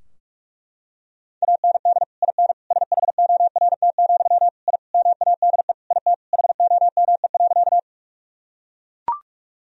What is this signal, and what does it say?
Morse code, heard as a keyed tone, about 42 wpm, 700 Hz. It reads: AND IN SHORT, I MADE IT HOME.
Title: Sentences from Top 1000 Words - Review Part 2 - 45wpm